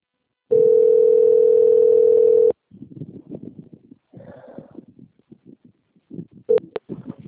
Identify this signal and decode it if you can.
Ringback tone (North America), 440 + 480 Hz.